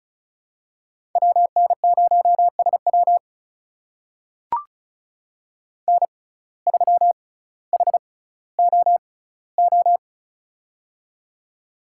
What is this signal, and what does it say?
Morse code, heard as a keyed tone, about 36 wpm, spaced as for 18 wpm, 700 Hz. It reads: WN0SW N3HOO
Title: USA Call Signs - 5 Characters - 6x Character Spacing - 35wpm